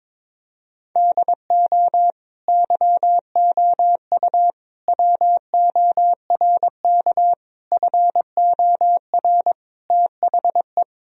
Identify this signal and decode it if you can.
Morse code, heard as a keyed tone, about 22 wpm, 700 Hz. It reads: DO YOU WORK FOR THE
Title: Sentences from Top 1000 Words - Review Part 4 - 22wpm